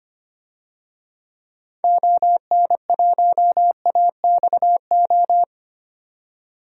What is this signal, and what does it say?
Morse code, heard as a keyed tone, about 25 wpm, 700 Hz. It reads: ON1AXO
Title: DXCC-100 Call Signs - 6 Characters - 25wpm